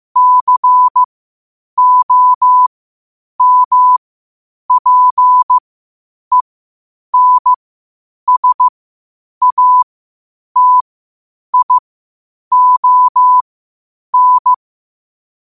Morse code, heard as a keyed tone, about 15 wpm, spaced as for 8 wpm, 1000 Hz. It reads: COMPENSATION
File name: qotdfiles / 8wpm